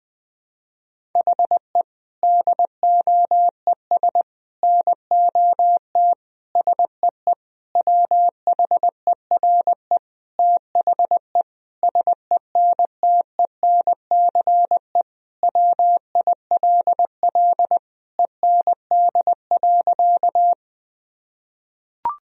Morse code, heard as a keyed tone, about 20 wpm, 700 Hz. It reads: HE DOES NOT SEE WHERE THE SENTENCE WILL END.
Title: Sentences from Top 1000 Words - Review Part 1 - 20wpm